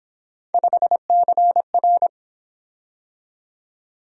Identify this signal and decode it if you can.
Morse code, heard as a keyed tone, about 27 wpm, 700 Hz. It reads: 5CR
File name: Speed-Racing - Three Letter-Number Combinations - 22wpm